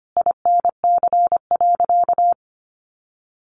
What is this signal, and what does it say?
Morse code, heard as a keyed tone, about 25 wpm, 700 Hz. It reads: INC.
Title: A Sound Of Thunder 25wpm